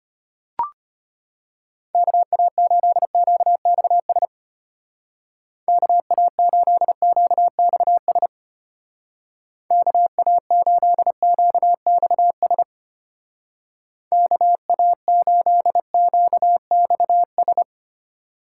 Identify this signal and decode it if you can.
Morse code, heard as a keyed tone, about 31 wpm, 700 Hz. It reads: KA8QXS KA8QXS KA8QXS KA8QXS
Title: Speed Racing - USA 2x3 Call Signs 25wpm